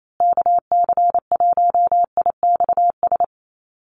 Morse code, heard as a keyed tone, about 28 wpm, 700 Hz. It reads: KC1IXS
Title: USA Call Signs Repeated 3x 28wpm